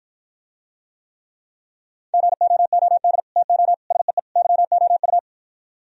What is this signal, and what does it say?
Morse code, heard as a keyed tone, about 56 wpm, 700 Hz. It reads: GOOD TO SEE YOU